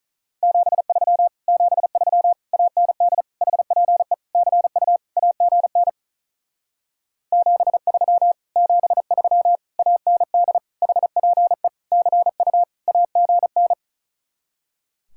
Morse code, heard as a keyed tone, about 43 wpm, 700 Hz. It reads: 73 73 AND HPE CU AGN 73 73 AND HPE CU AGN
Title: Speed-Racing Common QSO Phrases - 35wpm